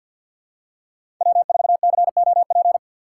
Morse code, heard as a keyed tone, about 51 wpm, 700 Hz. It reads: W4COP
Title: USA Call Signs - 5 Characters - 4x Character Spacing - 50wpm